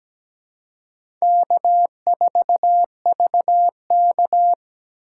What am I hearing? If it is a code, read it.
Morse code, heard as a keyed tone, about 17 wpm, 700 Hz. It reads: K4VK
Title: USA Call Signs - 4 Characters - 2x Character Spacing - 17wpm